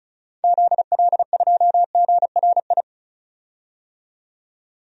Morse code, heard as a keyed tone, about 36 wpm, 700 Hz. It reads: ZL2GRI
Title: DXCC-100 Call Signs - 6 Characters - 35wpm